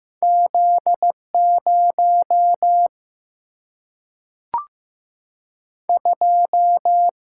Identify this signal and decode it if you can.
Morse code, heard as a keyed tone, about 16 wpm, 700 Hz. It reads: Z 0 2
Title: Single-Letter Single-Number 15wpm